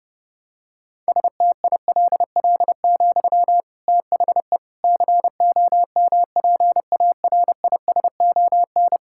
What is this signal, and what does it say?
Morse code, heard as a keyed tone, about 30 wpm, 700 Hz. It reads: STILL, THE COMPARISON
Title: Sentences from Top 1600 Words 30wpm